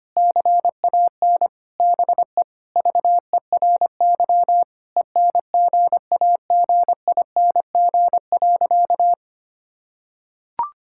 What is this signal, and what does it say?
Morse code, heard as a keyed tone, about 25 wpm, 700 Hz. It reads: CAN BE VERY ENGAGING.
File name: practice / 25wpm / Sentences with 2-Word Phrases - 25wpm